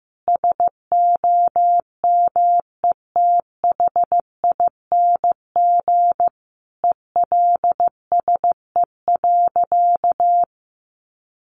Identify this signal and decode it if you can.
Morse code, heard as a keyed tone, about 14 wpm, 700 Hz. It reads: SOMETHING ELSE.
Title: Sentences from Top 800 Words 15wpm